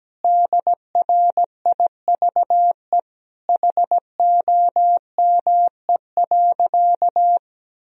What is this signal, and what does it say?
Morse code, heard as a keyed tone, about 17 wpm, 700 Hz. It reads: DRIVE HOME.
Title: Sentences from Top 1000 Words - Review Part 2 - 17wpm